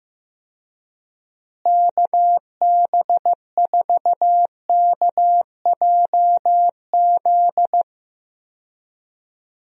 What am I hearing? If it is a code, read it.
Morse code, heard as a keyed tone, about 15 wpm, 700 Hz. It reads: KB4KJZ
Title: USA Call Signs - 6 Characters - Repeated 3x - 15wpm